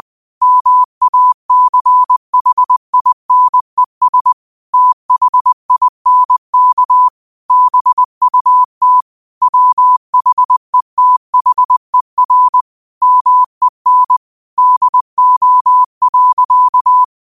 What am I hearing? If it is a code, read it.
Morse code, heard as a keyed tone, about 20 wpm, 1000 Hz. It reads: MACHINES THINK BUT WHETHER MEN DO.